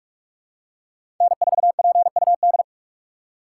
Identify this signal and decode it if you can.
Morse code, heard as a keyed tone, about 44 wpm, 690 Hz. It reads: N4PUD
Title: USA Call Signs - 5 Characters - 6x Character Spacing - 45wpm